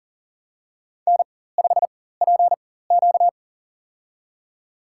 Morse code, heard as a keyed tone, about 37 wpm, spaced as for 18 wpm, 700 Hz. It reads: N5PQ
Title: USA Call Signs - 4x Character Spacing - 40wpm